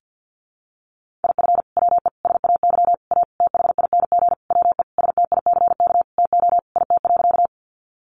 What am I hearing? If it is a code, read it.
Morse code, heard as a keyed tone, about 50 wpm, 700 Hz. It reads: IF WE SAY A THING WE STICK TO IT.
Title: Sentences from Top 1200 Words 50wpm